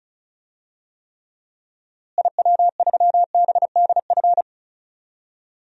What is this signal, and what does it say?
Morse code, heard as a keyed tone, about 36 wpm, 700 Hz. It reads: IW3BDF
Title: DXCC-100 Call Signs - 6 Characters - Repeated 3x - 35wpm